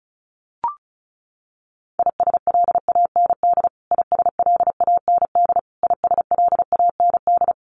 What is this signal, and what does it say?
Morse code, heard as a keyed tone, about 34 wpm, 700 Hz. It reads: ISLAND ISLAND ISLAND
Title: Top 500 Words Repeated 35wpm